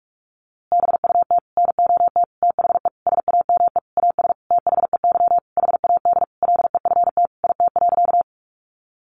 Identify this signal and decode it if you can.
Morse code, heard as a keyed tone, about 44 wpm, 700 Hz. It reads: BUT NOT THE SAME AS THEY HAD LEFT IT.
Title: A Sound Of Thunder 45wpm